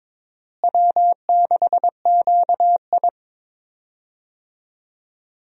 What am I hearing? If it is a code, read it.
Morse code, heard as a keyed tone, about 22 wpm, 700 Hz. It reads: W6QI
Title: USA Call Signs - 4 Characters - Repeated 3x - 22wpm